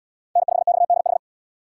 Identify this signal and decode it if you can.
Morse code, heard as a keyed tone, about 69 wpm, 700 Hz. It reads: N5XDH